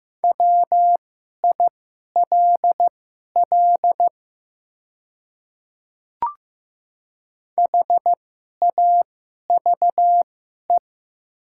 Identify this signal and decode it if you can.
Morse code, heard as a keyed tone, about 15 wpm, spaced as for 11 wpm, 700 Hz. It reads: WILL HAVE
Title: 2x Character Spacing - Top 100 Words - 15wpm